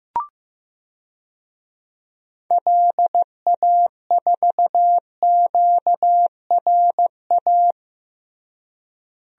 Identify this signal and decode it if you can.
Morse code, heard as a keyed tone, about 16 wpm, 700 Hz. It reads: LA4QRA
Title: DXCC-100 Call Signs - 6 Characters - Repeated 3x - 15wpm